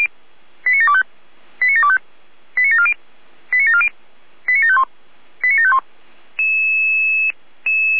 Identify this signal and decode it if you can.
Five-tone selective call (ZVEI), heard as quick runs of five tones, tones of about 70 ms. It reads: 89725 89725 89740 89740 89751 89751